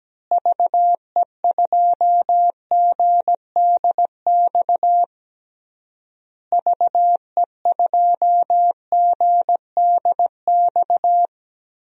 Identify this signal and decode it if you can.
Morse code, heard as a keyed tone, about 17 wpm, 700 Hz. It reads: VE2GDX VE2GDX
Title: DXCC-100 Call Signs - 6 Characters - Repeated 3x - 17wpm